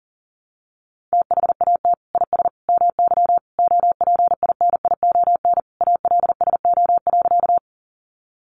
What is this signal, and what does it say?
Morse code, heard as a keyed tone, about 40 wpm, 700 Hz. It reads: THAT IS MY OPINION ALSO.
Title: Sentences from Top 1200 Words 40wpm